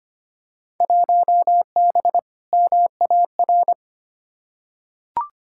Morse code, heard as a keyed tone, about 25 wpm, 700 Hz. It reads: 1B MAR
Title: Field Day Exchanges Encore - 25wpm